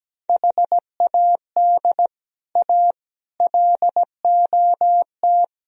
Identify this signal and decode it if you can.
Morse code, heard as a keyed tone, about 18 wpm, 700 Hz. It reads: HAD A LOT
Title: Speed-Racing - Sets of 4 Words 17wpm